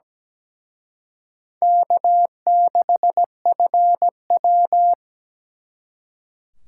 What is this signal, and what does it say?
Morse code, heard as a keyed tone, about 20 wpm, 700 Hz. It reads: K6FW
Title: Speed Racing - USA 1x2 Call Signs 17wpm